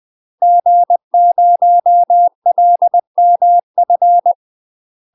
Morse code, heard as a keyed tone, about 20 wpm, 700 Hz. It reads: G0LMF
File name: UK MASTER QSOs TRAINING / 20WPM / UK MASTER QSOs 0022